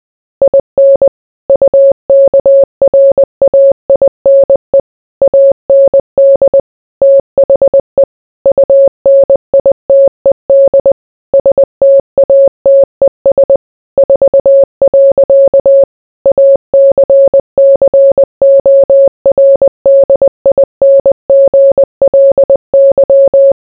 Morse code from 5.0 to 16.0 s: AND THE UNITED STATES 4.